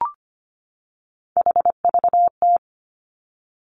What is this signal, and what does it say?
Morse code, heard as a keyed tone, about 25 wpm, 700 Hz. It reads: HVT
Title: Three Challenging Letter-Number Combinations 25wpm